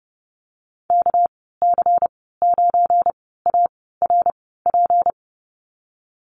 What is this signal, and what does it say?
Morse code, heard as a keyed tone, about 30 wpm, spaced as for 17 wpm, 700 Hz. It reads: KC9ARP
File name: Lesson 37g - Letter x - Call Signs 30wpm